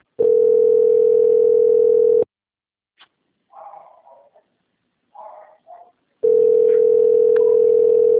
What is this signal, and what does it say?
Ringback tone (North America), 440 + 480 Hz.